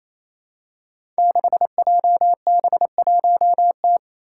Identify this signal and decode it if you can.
Morse code, heard as a keyed tone, about 29 wpm, 700 Hz. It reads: 6JB1T
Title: Five Challenging Letter-Number Combinations 28wpm